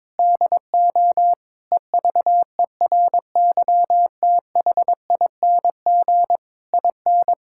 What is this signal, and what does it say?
Morse code, heard as a keyed tone, about 22 wpm, 700 Hz. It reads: DO EVERYTHING IN
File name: practice / 22wpm / Sets of 8 Words - 22wpm